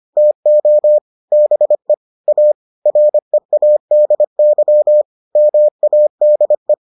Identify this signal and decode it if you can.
Morse code, heard as a keyed tone, about 25 wpm, 600 Hz. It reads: TO BE A READY MADE